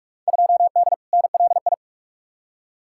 Morse code, heard as a keyed tone, about 43 wpm, 700 Hz. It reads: 2D NLI
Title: Field Day Exchanges - 45wpm